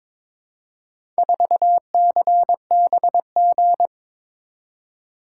Morse code, heard as a keyed tone, about 22 wpm, 700 Hz. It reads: 4CBG
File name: practice / 22wpm / Four Letter-Number Combinations - 22wpm